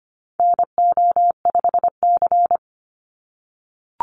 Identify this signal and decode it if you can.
Morse code, heard as a keyed tone, about 25 wpm, 700 Hz. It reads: NO5C
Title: USA Call Signs 25wpm